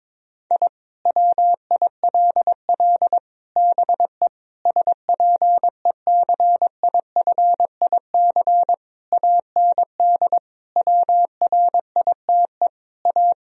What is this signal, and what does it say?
Morse code, heard as a keyed tone, about 22 wpm, 700 Hz. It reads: I WILL BE SPECIFIC AND WRITE A
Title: Sentences from Top 1000 Words - Review Part 3 - 22wpm